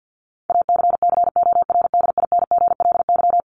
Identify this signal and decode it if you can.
Morse code, heard as a keyed tone, about 50 wpm, 700 Hz. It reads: ACCORDINGLY
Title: LY Suffix - Top 100 Words with Base Words 50wpm